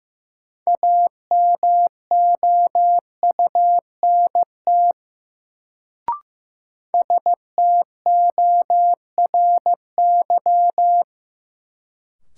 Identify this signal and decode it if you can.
Morse code, heard as a keyed tone, about 15 wpm, 700 Hz. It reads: AMOUNT STORY
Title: Sets of 3 Words - New Word Drill - 15wpm